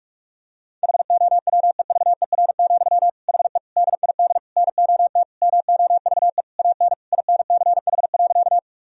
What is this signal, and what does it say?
Morse code, heard as a keyed tone, about 46 wpm, 700 Hz. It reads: HOWEVER, HE DID NOT MOVE AN INCH.
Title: Sentences from Top 1000 Words - Review Part 3 - 45wpm